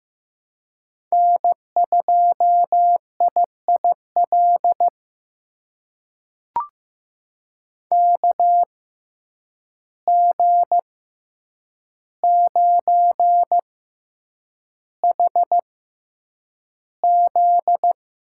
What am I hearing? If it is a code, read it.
Morse code, heard as a keyed tone, about 15 wpm, spaced as for 8 wpm, 700 Hz. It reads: N2IIL KG9HZ
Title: USA Call Signs - 5 Characters - 6x Character Spacing - 15wpm